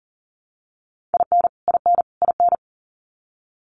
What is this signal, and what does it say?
Morse code, heard as a keyed tone, about 40 wpm, 700 Hz. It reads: IN IN IN
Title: Top 500 Words Repeated 40wpm